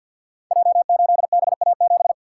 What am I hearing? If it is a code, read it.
Morse code, heard as a keyed tone, about 48 wpm, 700 Hz. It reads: J8BA7
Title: Code Groups - 5 Letter:Numbers - 50wpm